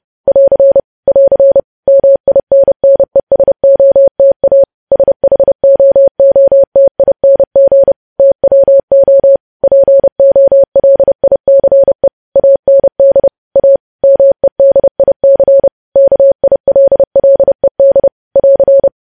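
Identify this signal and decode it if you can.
Morse code, heard as a keyed tone, about 30 wpm, 550 Hz. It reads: <AR> <AR> MINNESOTA SHOOTING TWO POLICE AND A MEDIC KILLED <AR>
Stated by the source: Morse code news headlines sent at 30.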